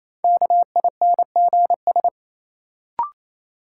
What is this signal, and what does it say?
Morse code, heard as a keyed tone, about 27 wpm, 700 Hz. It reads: KINGS
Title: Sentences from Top 1000 Words - Review Part 6 - New Word Drill - 28wpm